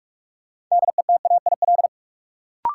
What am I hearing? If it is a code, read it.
Morse code, heard as a keyed tone, about 44 wpm, 700 Hz. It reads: DETAIL
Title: Sentences from Top 1500 Words - New Word Drill Alt - 45wpm